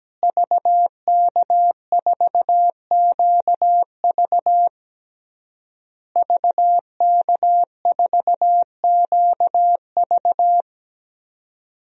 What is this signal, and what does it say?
Morse code, heard as a keyed tone, about 17 wpm, 700 Hz. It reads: VK4QV VK4QV
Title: DXCC-100 Call Signs - 5 Characters - Repeated 3x - 17wpm